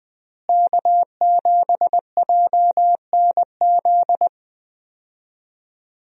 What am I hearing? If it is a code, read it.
Morse code, heard as a keyed tone, about 20 wpm, 700 Hz. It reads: K7JNZ
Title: USA Call Signs - 5 Characters - Repeated 3x - 20wpm